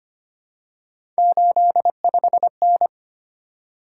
Morse code, heard as a keyed tone, about 25 wpm, 700 Hz. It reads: 85N
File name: Three Letter-Number Combinations - 25wpm